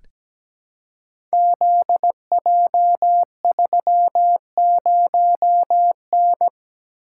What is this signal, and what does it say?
Morse code, heard as a keyed tone, about 17 wpm, 700 Hz. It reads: ZJ30N
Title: Code Groups - 5 Letter:Numbers - 17wpm